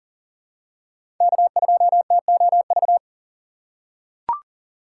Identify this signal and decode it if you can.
Morse code, heard as a keyed tone, about 40 wpm, 700 Hz. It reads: K2TOV
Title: USA Call Signs - 5 Characters - 6x Character Spacing - 40wpm